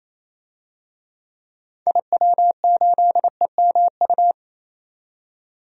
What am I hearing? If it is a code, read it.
Morse code, heard as a keyed tone, about 28 wpm, 700 Hz. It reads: IW8EMU
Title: DXCC-100 Call Signs - 6 Characters - 28wpm